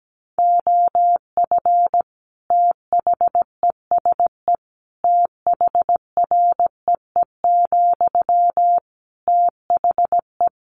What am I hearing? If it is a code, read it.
Morse code, heard as a keyed tone, about 17 wpm, 700 Hz. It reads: OF THESE THREE, THE
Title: Sentences from Top 700 Words 17wpm